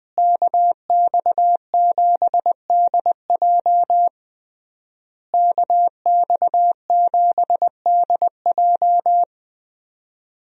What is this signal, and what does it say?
Morse code, heard as a keyed tone, about 20 wpm, 700 Hz. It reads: KX7DJ KX7DJ
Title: USA Call Signs - 5 Characters - Repeated 3x - 20wpm